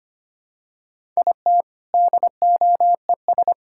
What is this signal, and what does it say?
Morse code, heard as a keyed tone, about 25 wpm, 690 Hz. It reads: IT DOES
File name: practice / 25wpm / Sentences with 2-Word Phrases - 25wpm